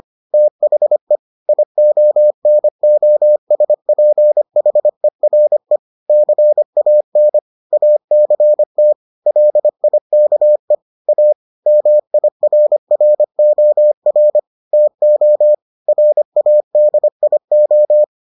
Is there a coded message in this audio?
Yes: Morse code from 0.3 to 18.0 s, heard as a keyed tone, about 25 wpm, 600 Hz. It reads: THE IONOSPHERE CAN ACT LIKE A MIRROR TO RADIO